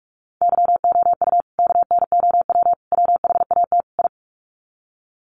Morse code, heard as a keyed tone, about 44 wpm, 700 Hz. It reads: YOU KNOW WHAT I
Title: Sets of 4 Words 45wpm